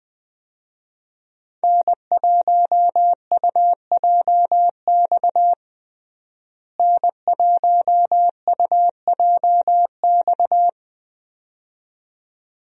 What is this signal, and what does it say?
Morse code, heard as a keyed tone, about 20 wpm, 700 Hz. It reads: N1UJX N1UJX
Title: USA Call Signs - 5 Characters - Repeated 2x - 20wpm